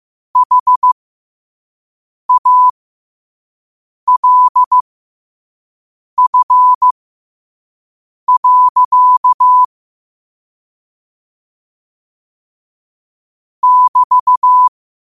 Morse code, heard as a keyed tone, about 15 wpm, spaced as for 5 wpm, 1000 Hz. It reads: HALF. <BT>
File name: qotdfiles / 5wpm